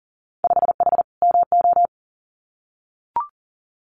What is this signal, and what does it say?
Morse code, heard as a keyed tone, about 40 wpm, 700 Hz. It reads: 5H MO
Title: Winter Field Day Exchanges Encore 40wpm